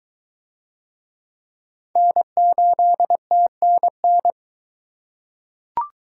Morse code, heard as a keyed tone, about 23 wpm, 700 Hz. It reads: N8TNN